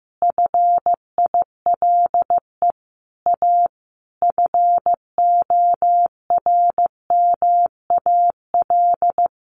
Morse code, heard as a keyed tone, about 15 wpm, 700 Hz. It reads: FILE A FORMAL
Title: Sentences from Top 1300 Words 15wpm